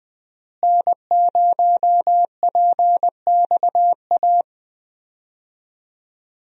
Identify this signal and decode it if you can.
Morse code, heard as a keyed tone, about 20 wpm, 700 Hz. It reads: N0PXA
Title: USA Call Signs - 5 Characters - Repeated 3x - 20wpm